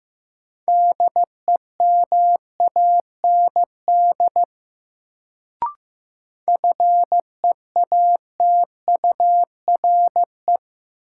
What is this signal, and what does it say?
Morse code, heard as a keyed tone, about 15 wpm, 700 Hz. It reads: DEMAND FEATURE
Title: Sentences from Top 800 Words - New Word Drill - 15wpm